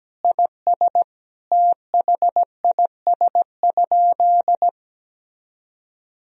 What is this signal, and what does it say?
Morse code, heard as a keyed tone, about 17 wpm, 700 Hz. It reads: IS THIS?
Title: Sentences from Top 1000 Words - Review Part 1 - 17wpm